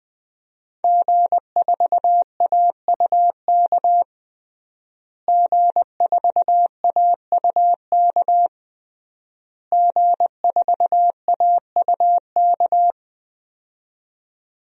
Morse code, heard as a keyed tone, about 20 wpm, 700 Hz. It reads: G4AUK G4AUK G4AUK
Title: DXCC-100 Call Signs - 5 Characters - Repeated 3x - 20wpm